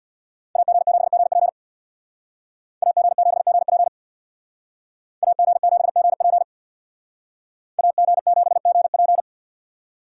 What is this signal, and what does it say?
Morse code, heard as a keyed tone, about 62 wpm, 700 Hz. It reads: AG7GP AG7GP AG7GP AG7GP
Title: Speed Racing - USA Call Signs 50wpm